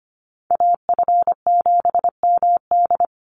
Morse code, heard as a keyed tone, about 25 wpm, 700 Hz. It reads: AF7MD
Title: NA SOTA Chaser Callsigns 25wpm